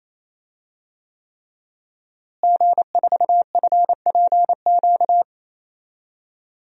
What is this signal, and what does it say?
Morse code, heard as a keyed tone, about 28 wpm, 700 Hz. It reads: G4FPQ
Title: DXCC-100 Call Signs - 5 Characters - Repeated 3x - 28wpm